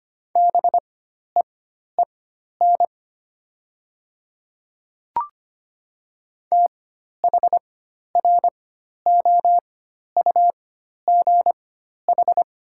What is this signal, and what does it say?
Morse code, heard as a keyed tone, about 25 wpm, spaced as for 12 wpm, 700 Hz. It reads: BEEN THROUGH
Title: 4x Character Spacing - Top 200 Words - 25wpm